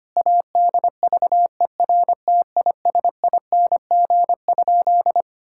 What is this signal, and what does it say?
Morse code, heard as a keyed tone, about 25 wpm, 700 Hz. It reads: ADVERTISING?
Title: Sentences from Top 1000 Words - Review Part 6 - 25wpm